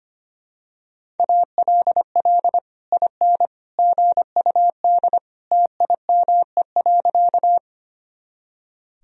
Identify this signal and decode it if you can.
Morse code, heard as a keyed tone, about 25 wpm, 700 Hz. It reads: ALL IN GUD TIME.
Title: Sentences from Top 200 Words and Abbreviations 25wpm